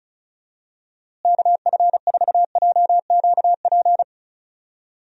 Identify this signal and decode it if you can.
Morse code, heard as a keyed tone, about 36 wpm, 700 Hz. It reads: KF4JQP
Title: USA Call Signs - 6 Characters - Repeated 2x - 35wpm